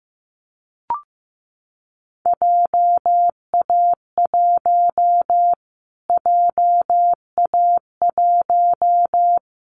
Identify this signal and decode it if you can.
Morse code, heard as a keyed tone, about 15 wpm, 700 Hz. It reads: JA1 JA1
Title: Callsign Prefixes for Top 25 DXCC - Repeated 3x - 15wpm